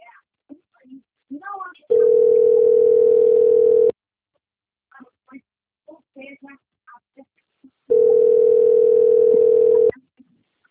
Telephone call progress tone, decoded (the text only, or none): ringback tone